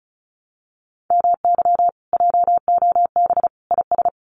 Morse code, heard as a keyed tone, about 34 wpm, 700 Hz. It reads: MY JOB IS
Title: Common QSO Phrases Encore 35wpm